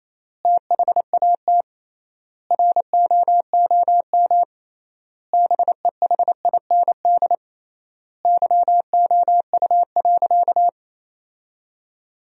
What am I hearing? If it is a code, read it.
Morse code, heard as a keyed tone, about 28 wpm, 700 Hz. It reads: THAT ROOM BEHIND YOU.
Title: 3x Word-Spacing - Sentences from Top 500 Words - 28wpm